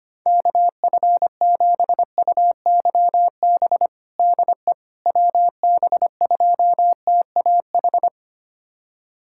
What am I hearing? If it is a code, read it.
Morse code, heard as a keyed tone, about 25 wpm, 700 Hz. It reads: KF7UYB DE WB2TAH
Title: USA Callsign DE Callsign - 6 Characters - 25wpm